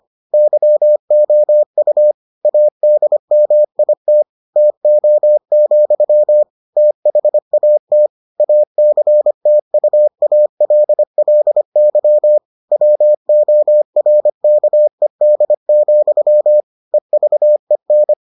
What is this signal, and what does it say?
Morse code, heard as a keyed tone, about 25 wpm, 600 Hz. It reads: YOU ADMIT TO, THAT ACTUALLY WORKED, EVEN